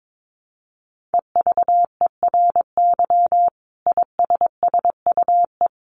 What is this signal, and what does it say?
Morse code, heard as a keyed tone, about 22 wpm, 700 Hz. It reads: EVERY ISSUE
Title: Sentences from Top 1400 Words 22wpm